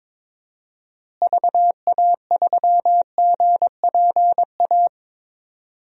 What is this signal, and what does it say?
Morse code, heard as a keyed tone, about 22 wpm, 700 Hz. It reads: VA3GPA
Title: DXCC-100 Call Signs - 6 Characters - Repeated 2x - 22wpm